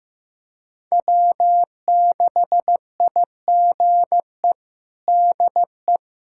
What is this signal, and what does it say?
Morse code, heard as a keyed tone, about 15 wpm, 700 Hz. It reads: W6IGE DE
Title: USA Callsign DE Callsign - 5 Characters - 15wpm